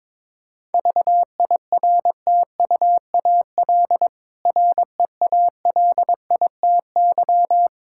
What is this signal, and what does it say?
Morse code, heard as a keyed tone, about 22 wpm, 690 Hz. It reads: VIRTUAL REALITY
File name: Difficult Words with Hints - 22wpm